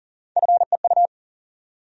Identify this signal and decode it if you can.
Morse code, heard as a keyed tone, about 40 wpm, 700 Hz. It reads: FEU
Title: USA Call Sign Postfixes - 3 Letters 40wpm